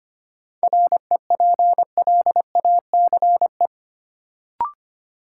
Morse code, heard as a keyed tone, about 25 wpm, 700 Hz. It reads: REPLACE
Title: Sentences from Top 1400 Words - New Word Drill Alt - 25wpm